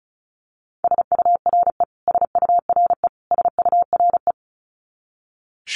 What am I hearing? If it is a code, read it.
Morse code, heard as a keyed tone, about 34 wpm, 700 Hz. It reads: SURE SURE SURE
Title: Top 500 Words Repeated 35wpm